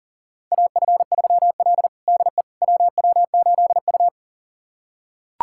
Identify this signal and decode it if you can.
Morse code, heard as a keyed tone, about 40 wpm, 700 Hz. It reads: AF3L DE WW8U
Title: USA Callsign DE Callsign - 4 Characters - 40wpm